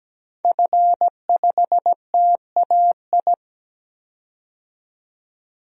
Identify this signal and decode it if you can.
Morse code, heard as a keyed tone, about 17 wpm, 700 Hz. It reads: F5TAI
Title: DXCC-100 Call Signs - 5 Characters - 17wpm